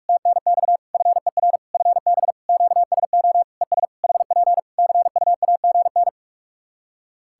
Morse code, heard as a keyed tone, about 46 wpm, 700 Hz. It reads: TNX FER FB QSO ES HP CUAGN
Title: K1USN SST QSO Practice 45wpm